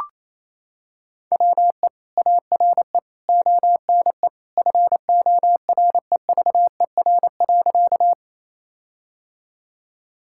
Morse code, heard as a keyed tone, about 28 wpm, 700 Hz. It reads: WE ARE ONE FOREVER.
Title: Sentences from Top 1600 Words 28wpm